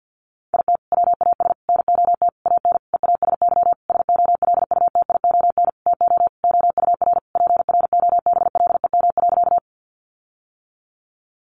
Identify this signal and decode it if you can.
Morse code, heard as a keyed tone, about 50 wpm, 700 Hz. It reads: IT WAS NOT AN EASY SOLUTION TO OUR PROBLEM.
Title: Sentences from Top 1300 Words 50wpm